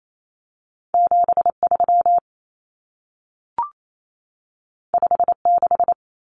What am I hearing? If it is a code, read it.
Morse code, heard as a keyed tone, about 28 wpm, 700 Hz. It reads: 73 56
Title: Numbers - 2-Digit 28wpm